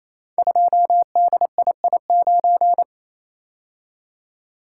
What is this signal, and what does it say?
Morse code, heard as a keyed tone, about 28 wpm, 700 Hz. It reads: 2DII9
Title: Code Groups - 5 Letter:Numbers - 28wpm